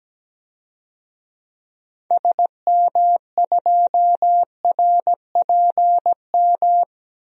Morse code, heard as a keyed tone, about 17 wpm, 700 Hz. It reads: SM2RPM
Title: DXCC-100 Call Signs - 6 Characters - Repeated 3x - 17wpm